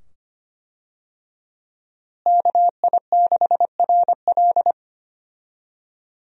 Morse code, heard as a keyed tone, about 25 wpm, 700 Hz. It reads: KI6RL
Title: USA Call Signs - 5 Characters - Repeated 2x - 25wpm